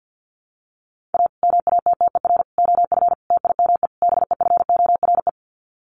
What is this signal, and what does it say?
Morse code, heard as a keyed tone, about 50 wpm, 700 Hz. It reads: A MATTER OF TIME BEFORE
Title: Sets of 5 Words Encore 50wpm